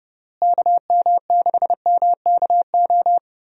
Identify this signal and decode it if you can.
Morse code, heard as a keyed tone, about 32 wpm, 700 Hz. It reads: KM6MKO
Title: USA Call Signs - 6 Characters - Repeated 3x - 30wpm